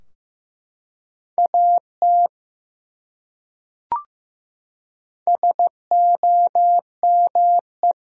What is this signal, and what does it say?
Morse code, heard as a keyed tone, about 16 wpm, 700 Hz. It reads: AT SOME
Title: Top 100 Words Repeated 15wpm